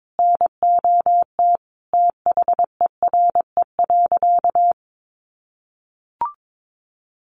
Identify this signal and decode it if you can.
Morse code, heard as a keyed tone, about 22 wpm, 700 Hz. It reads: NOT THERE.
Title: Sentences from Top 1400 Words 22wpm